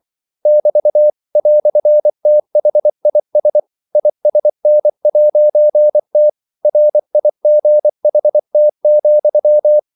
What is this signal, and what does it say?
Morse code, heard as a keyed tone, about 24 wpm, 600 Hz. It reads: <BT> "THIS ISN'T RIGHT,
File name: CW / Practice / EnglishQuotations0424